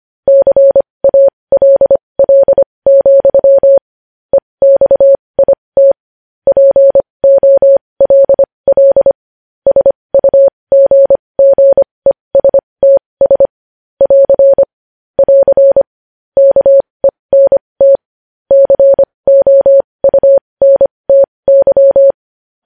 Morse code, heard as a keyed tone, about 25 wpm, spaced as for 20 wpm, 550 Hz. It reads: CALL, EXIT POLL SUGGESTS <AR> <AR> KENT COUNTY
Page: news-headlines-in-morse-code-at-20-wpm